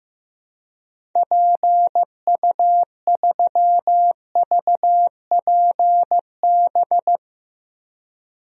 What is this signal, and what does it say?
Morse code, heard as a keyed tone, about 15 wpm, 700 Hz. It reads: PU3VPB
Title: DXCC-100 Call Signs - 6 Characters - 15wpm